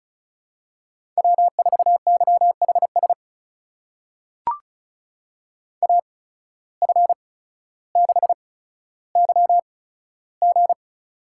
Morse code, heard as a keyed tone, about 36 wpm, spaced as for 15 wpm, 700 Hz. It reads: W4YHS AF6YG
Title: USA Call Signs - 5 Characters - 8x Character Spacing - 35wpm